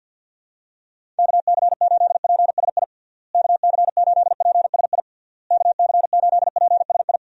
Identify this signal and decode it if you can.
Morse code, heard as a keyed tone, about 53 wpm, 700 Hz. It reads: KC8PSI KC8PSI KC8PSI
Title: USA Call Signs - 6 Characters - Repeated 3x - 50wpm